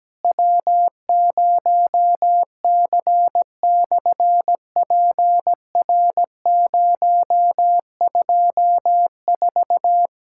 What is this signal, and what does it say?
Morse code, heard as a keyed tone, about 17 wpm, 700 Hz. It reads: W0C/PR024
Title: SOTA QSO Practice 17wpm